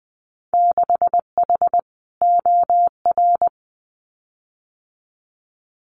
Morse code, heard as a keyed tone, about 20 wpm, 700 Hz. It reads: 6H OR
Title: Winter Field Day Exchanges Encore 20wpm